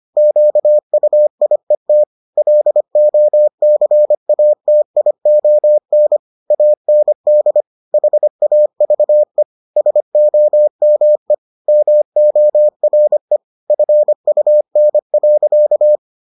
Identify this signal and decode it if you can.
Morse code, heard as a keyed tone, about 25 wpm, 600 Hz. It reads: QUIET LOCATION AND HAVE SOME MORE FUN.